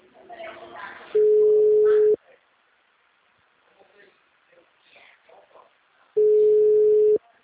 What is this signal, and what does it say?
Ringback tone (Europe, 425 Hz), with speech.